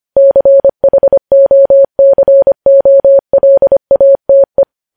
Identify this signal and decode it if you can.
Morse code, heard as a keyed tone, about 25 wpm, 550 Hz. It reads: CHOCOLATE
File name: podcast / News headlines-1x-25x25